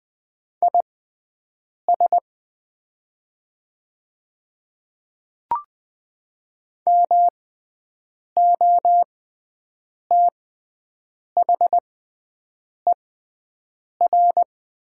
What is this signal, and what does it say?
Morse code, heard as a keyed tone, about 20 wpm, spaced as for 7 wpm, 700 Hz. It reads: IS MOTHER
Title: 6x Character Spacing - Top 300 Words - 20wpm